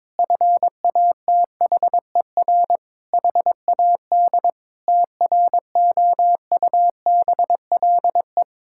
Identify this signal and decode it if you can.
Morse code, heard as a keyed tone, about 22 wpm, 700 Hz. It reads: FATHER HAD TROUBLE
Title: Sentences from Top 1600 Words 22wpm